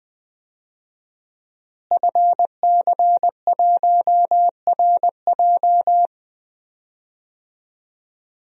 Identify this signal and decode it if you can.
Morse code, heard as a keyed tone, about 20 wpm, 700 Hz. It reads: FC1RJ
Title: DXCC-100 Call Signs - 5 Characters - Repeated 2x - 20wpm